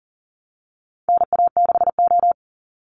Morse code, heard as a keyed tone, about 41 wpm, 700 Hz. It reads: NA6O
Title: Call Signs - 40wpm